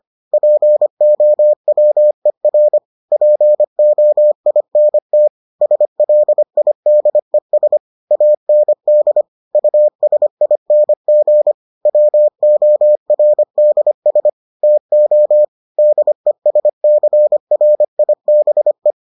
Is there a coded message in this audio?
Yes: Morse code from 0.3 to 18.9 s, heard as a keyed tone, about 25 wpm, 600 Hz. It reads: POWER POINT SLIDES AND USING WORDS TO DESCRIBE